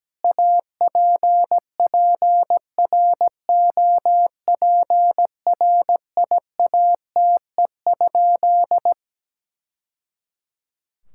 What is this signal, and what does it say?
Morse code, heard as a keyed tone, about 17 wpm, 700 Hz. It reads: APPROPRIATE?
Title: Sentences from Top 1000 Words - Review Part 5 - 17wpm